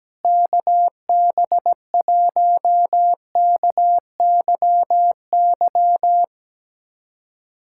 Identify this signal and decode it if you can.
Morse code, heard as a keyed tone, about 17 wpm, 700 Hz. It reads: KB1KYY
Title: USA Callsign DE Callsign - 6 Characters - 17wpm